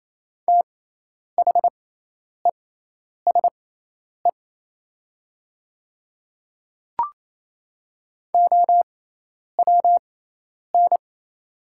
Morse code, heard as a keyed tone, about 27 wpm, spaced as for 10 wpm, 700 Hz. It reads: THESE OWN
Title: 6x Character Spacing - Top 300 Words - 28wpm